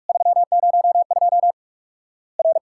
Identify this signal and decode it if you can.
Morse code, heard as a keyed tone, about 46 wpm, 700 Hz. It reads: 302 R
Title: Worked All Europe DX Contest - QTC Exchange - 45wpm